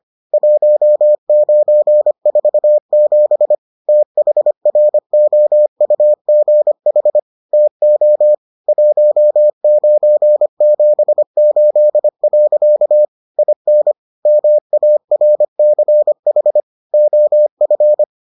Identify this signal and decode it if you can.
Morse code, heard as a keyed tone, about 25 wpm, 600 Hz. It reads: 1947 THROUGH TO 1978. IN MARCH OF